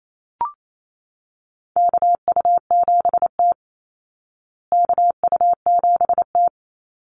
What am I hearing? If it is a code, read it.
Morse code, heard as a keyed tone, about 28 wpm, 700 Hz. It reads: KU7T KU7T
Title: Call Signs Repeated 3x - 28wpm